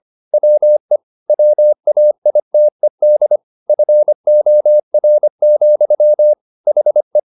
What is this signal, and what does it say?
Morse code, heard as a keyed tone, about 25 wpm, 600 Hz. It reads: WE WAITED FOR, HE